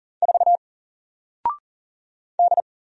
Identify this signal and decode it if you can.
Morse code, heard as a keyed tone, about 40 wpm, 700 Hz. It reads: 4 D
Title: Single Letter-Number Alt 40wpm